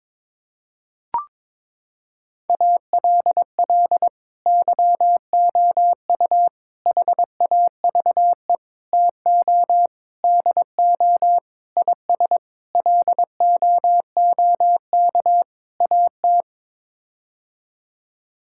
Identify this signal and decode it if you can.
Morse code, heard as a keyed tone, about 22 wpm, 700 Hz. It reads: ALL YOU HAVE TO DO IS LOOK AT
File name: practice / 22wpm / Sets of 8 Words - 22wpm